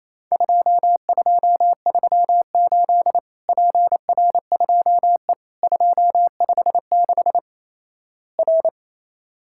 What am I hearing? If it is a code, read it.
Morse code, heard as a keyed tone, about 28 wpm, 700 Hz. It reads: 2238 PR2E 256 R
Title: Worked All Europe DX Contest - QTC Exchange - 28wpm